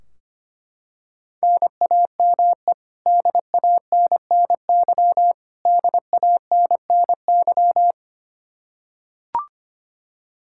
Morse code, heard as a keyed tone, about 25 wpm, 700 Hz. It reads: NAME DANNY DANNY
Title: Practice QSO - Exchange 1 - Beginning - 25wpm